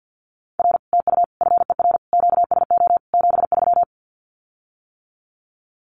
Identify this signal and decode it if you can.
Morse code, heard as a keyed tone, about 50 wpm, 700 Hz. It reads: R TU FER QSO 73
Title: Common QSO Phrases Encore 50wpm